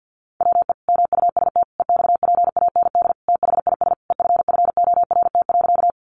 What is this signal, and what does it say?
Morse code, heard as a keyed tone, about 50 wpm, 700 Hz. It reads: WE MUST EXPAND THIS EFFORT.
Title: Sentences from Top 1400 Words 50wpm